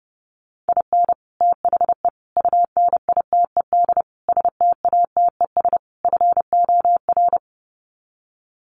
Morse code, heard as a keyed tone, about 30 wpm, 700 Hz. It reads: IN THE UNITED STATES FOR
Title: Sets of 5 Words Encore2 30wpm